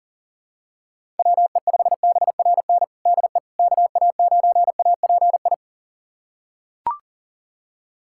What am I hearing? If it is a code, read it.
Morse code, heard as a keyed tone, about 40 wpm, 700 Hz. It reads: WE5BRN DE KA9API
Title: USA Callsign DE Callsign - 6 Characters - 40wpm